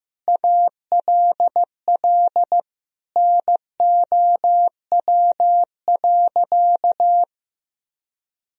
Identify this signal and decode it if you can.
Morse code, heard as a keyed tone, about 15 wpm, 700 Hz. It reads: ALL NOW.
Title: Sentences from Top 1000 Words - Review Part 1 - 15wpm